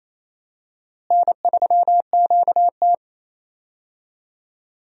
Morse code, heard as a keyed tone, about 27 wpm, 700 Hz. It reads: N3QT
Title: USA Call Signs - 4 Characters - Repeated 3x - 28wpm